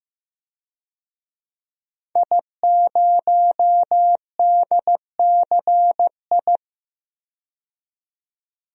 Morse code, heard as a keyed tone, about 15 wpm, 700 Hz. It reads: I0DCI
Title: DXCC-100 Call Signs - 5 Characters - 15wpm